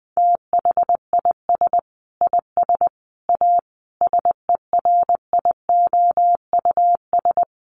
Morse code, heard as a keyed tone, about 19 wpm, 700 Hz. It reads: THIS IS A SERIOUS
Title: Sentences from Top 1100 Words 20wpm